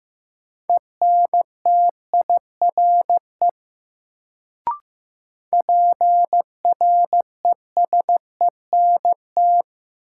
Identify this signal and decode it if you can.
Morse code, heard as a keyed tone, about 15 wpm, 700 Hz. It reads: ENTIRE PRESENT
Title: Sentences from Top 800 Words - New Word Drill - 15wpm